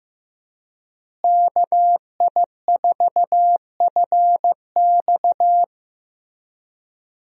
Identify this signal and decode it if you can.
Morse code, heard as a keyed tone, about 15 wpm, 700 Hz. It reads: KI4FX
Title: USA Call Signs - 5 Characters - Repeated 3x - 15wpm